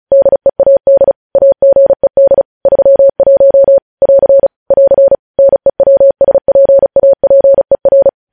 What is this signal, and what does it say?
Morse code, heard as a keyed tone, about 36 wpm, 550 Hz. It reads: DEAD AGED 31 <AR> <AR> NEWSPAPER